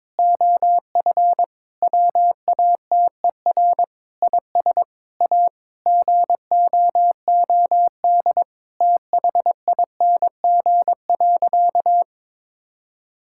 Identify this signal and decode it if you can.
Morse code, heard as a keyed tone, about 22 wpm, 700 Hz. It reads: OF WATER IS A GOOD THING.
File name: practice / 22wpm / Sentences with 3-Word Phrases - 22wpm